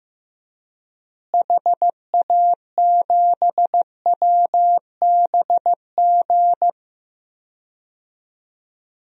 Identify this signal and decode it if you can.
Morse code, heard as a keyed tone, about 15 wpm, 700 Hz. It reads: HA7WBG
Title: DXCC-100 Call Signs - 6 Characters - Repeated 2x - 15wpm